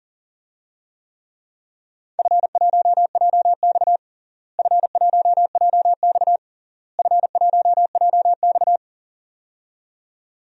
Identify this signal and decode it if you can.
Morse code, heard as a keyed tone, about 40 wpm, 700 Hz. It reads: F1JX F1JX F1JX
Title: DXCC-100 Call Signs - 4 Characters - Repeated 3x - 40wpm